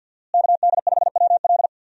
Morse code, heard as a keyed tone, about 52 wpm, 700 Hz. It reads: KD5WL
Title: USA Call Signs - 5 Characters - Repeated 2x - 50wpm